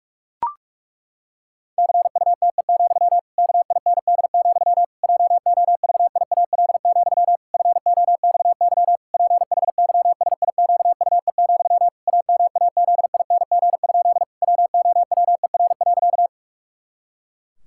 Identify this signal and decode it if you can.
Morse code, heard as a keyed tone, about 41 wpm, 700 Hz. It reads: CUTE, KIND, JOVIAL, FOXY PHYSIQUE, AMAZING? WOWER.
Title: Pangram Challenge v2 - 45wpm